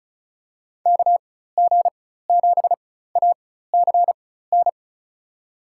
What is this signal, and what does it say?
Morse code, heard as a keyed tone, about 36 wpm, spaced as for 16 wpm, 700 Hz. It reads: KG7ACN
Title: USA Call Signs - 6 Characters - 4x Character Spacing - 35wpm